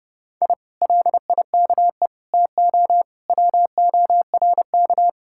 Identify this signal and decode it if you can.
Morse code, heard as a keyed tone, about 31 wpm, 700 Hz. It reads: I LIKE TO WORK